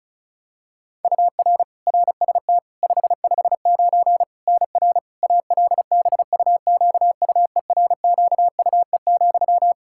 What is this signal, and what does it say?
Morse code, heard as a keyed tone, about 36 wpm, 700 Hz. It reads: UR RST 559 NR ALBUQUERQUE,
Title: Practice QSO - Exchange 1 - Beginning - 35wpm